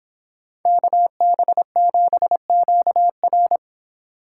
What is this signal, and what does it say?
Morse code, heard as a keyed tone, about 26 wpm, 700 Hz. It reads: KB7QR